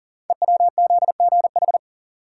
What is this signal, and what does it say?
Morse code, heard as a keyed tone, about 38 wpm, 700 Hz. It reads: EWZGH
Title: Code Groups - 5 Letters - 40wpm